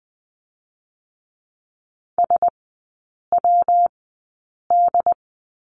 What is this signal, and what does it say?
Morse code, heard as a keyed tone, about 20 wpm, 700 Hz. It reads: S W D
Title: 114 Lesson 19e - Letter w - Three Characters 20wpm